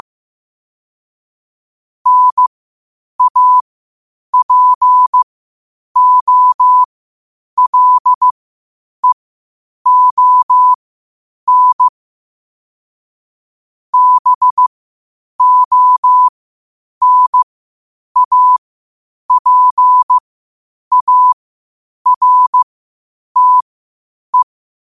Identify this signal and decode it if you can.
Morse code, heard as a keyed tone, about 15 wpm, spaced as for 8 wpm, 1000 Hz. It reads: NAPOLEON BONAPARTE
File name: qotdfiles / 8wpm